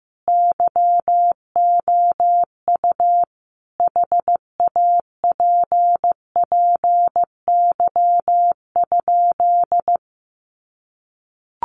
Morse code, heard as a keyed tone, about 15 wpm, 700 Hz. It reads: YOU HAPPY?
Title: Sentences from Top 1000 Words 15wpm